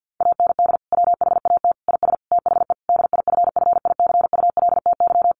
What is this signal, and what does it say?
Morse code, heard as a keyed tone, about 50 wpm, 700 Hz. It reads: AND WHAT IS THE DIFFICULTY
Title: Sentences from Top 1500 Words 50wpm